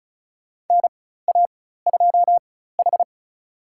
Morse code, heard as a keyed tone, about 36 wpm, spaced as for 16 wpm, 700 Hz. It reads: NA2H
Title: USA Call Signs - 4 Characters - 4x Character Spacing - 35wpm